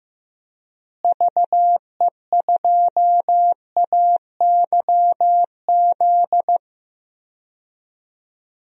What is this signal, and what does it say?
Morse code, heard as a keyed tone, about 15 wpm, 700 Hz. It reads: VE2AYZ
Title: DXCC-100 Call Signs - 6 Characters - Repeated 3x - 15wpm